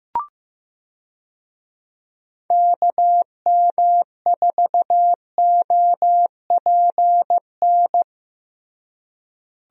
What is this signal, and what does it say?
Morse code, heard as a keyed tone, about 15 wpm, 700 Hz. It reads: KM4OPN
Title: USA Call Signs - 6 Characters - Repeated 2x - 15wpm